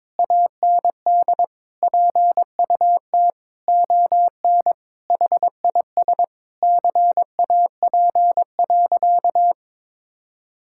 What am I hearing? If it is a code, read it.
Morse code, heard as a keyed tone, about 22 wpm, 700 Hz. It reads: AND PUT ON HIS CAP.
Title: Sentences from Top 1600 Words 22wpm